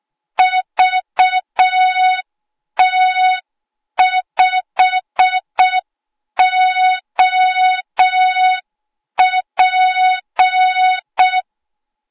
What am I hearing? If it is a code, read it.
Morse code, heard as a keyed tone, about 6 wpm, 3700 Hz. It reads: VT5OP
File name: Morze-HA2MN / hang / 08KXC HA2MN